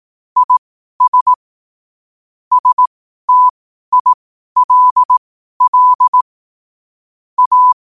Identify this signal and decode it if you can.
Morse code, heard as a keyed tone, about 18 wpm, spaced as for 12 wpm, 1000 Hz. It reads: IS STILL A
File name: qotdfiles / 12wpm